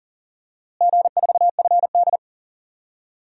Morse code, heard as a keyed tone, about 40 wpm, 700 Hz. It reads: G4FD
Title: DXCC-100 Call Signs - 4 Characters - Repeated 2x - 40wpm